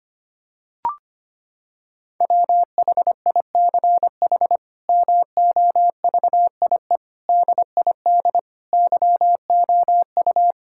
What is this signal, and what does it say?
Morse code, heard as a keyed tone, about 25 wpm, 700 Hz. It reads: WHICH MOVIE DID YOU
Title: Sentences from Top 1000 Words - Review Part 5 - 25wpm